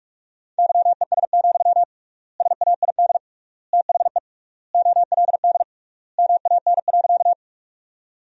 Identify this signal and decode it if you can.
Morse code, heard as a keyed tone, about 46 wpm, 700 Hz. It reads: YES, SAID THE OLD MAN.
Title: 3x Word-Spacing - Sentences from Top 500 Words - 45wpm